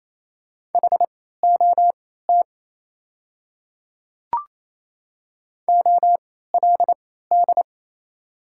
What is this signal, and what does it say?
Morse code, heard as a keyed tone, about 28 wpm, spaced as for 16 wpm, 700 Hz. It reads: HOT OLD
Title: 3x Character Spacing - Top 300 Words - 28wpm